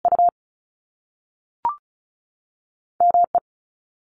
Morse code, heard as a keyed tone, about 36 wpm, 700 Hz. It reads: U M E
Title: Top Two Letter Combinations in Words 35wpm